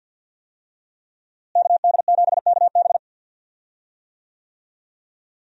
Morse code, heard as a keyed tone, about 52 wpm, 700 Hz. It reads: KD7KB
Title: USA Call Signs - 5 Characters - Repeated 2x - 50wpm